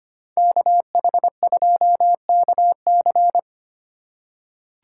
Morse code, heard as a keyed tone, about 25 wpm, 700 Hz. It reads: KH2KC
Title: USA Call Signs - 5 Characters - 4x Character Spacing - 25wpm